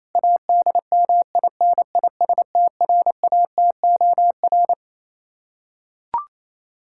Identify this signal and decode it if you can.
Morse code, heard as a keyed tone, about 28 wpm, 700 Hz. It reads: ADMINISTRATOR
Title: K1USN SST QSO Practice 28wpm